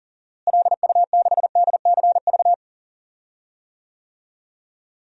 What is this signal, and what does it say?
Morse code, heard as a keyed tone, about 40 wpm, 700 Hz. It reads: LU6DCV
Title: DXCC-100 Call Signs - 6 Characters - Repeated 2x - 40wpm